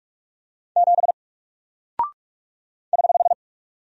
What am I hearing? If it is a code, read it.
Morse code, heard as a keyed tone, about 47 wpm, 700 Hz. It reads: 7 <HH>
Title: Single Character (Letter:Number:Punctuation:Prosign) Alt 45wpm